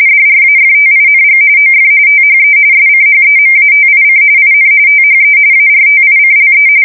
RTTY, about 45.45 baud. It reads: DRAFTING OF THE SCHEMATIC IN THE ARTICLE
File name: fsk170 / teletype-RTTY